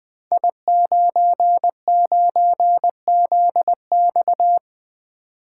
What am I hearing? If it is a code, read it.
Morse code, heard as a keyed tone, about 20 wpm, 700 Hz. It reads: I99ZX
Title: Code Groups - 5 Letter:Numbers - 20wpm